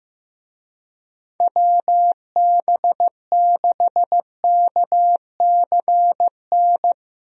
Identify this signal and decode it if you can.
Morse code, heard as a keyed tone, about 15 wpm, 700 Hz. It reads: WB6KCN
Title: USA Call Signs - 6 Characters - 15wpm